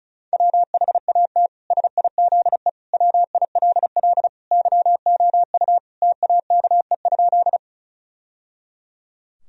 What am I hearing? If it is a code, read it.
Morse code, heard as a keyed tone, about 36 wpm, 700 Hz. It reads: WHAT SIZE WILL YOU TAKE?
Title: Sentences from Top 1000 Words - Review Part 2 - 35wpm